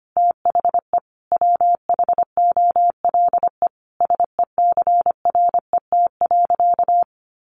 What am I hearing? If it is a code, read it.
Morse code, heard as a keyed tone, about 25 wpm, 700 Hz. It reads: THE WHOLE SECRET.
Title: Sentences from Top 1200 Words 25wpm